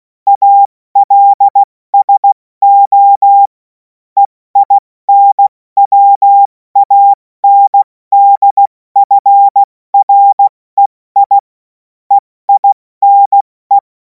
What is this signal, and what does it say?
Morse code, heard as a keyed tone, about 16 wpm, 800 Hz. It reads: ALSO EINWANDFREI EINE
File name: Morsekurs / Witze / Witz 24. Februar 2026 70 F80